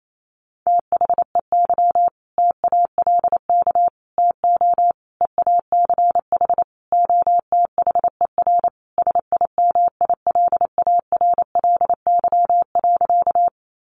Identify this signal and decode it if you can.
Morse code, heard as a keyed tone, about 28 wpm, 700 Hz. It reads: THEY TALK TO EACH OTHER SIMILARLY.
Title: Sentences from Top 1400 Words 28wpm